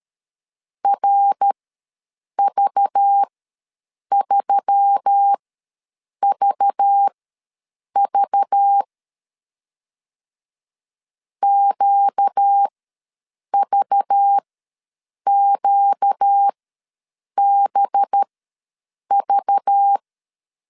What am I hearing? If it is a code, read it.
Morse code, heard as a keyed tone, about 13 wpm, spaced as for 7 wpm, 800 Hz. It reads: RV3VV QVQBV
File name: LEZIONI / 5 LBVC / 5 LBV ws 12wpm 3015